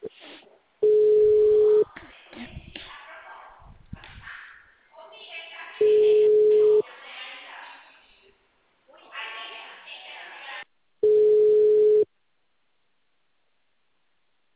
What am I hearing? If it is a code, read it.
Ringback tone (Europe, 425 Hz), with speech.
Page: Dracena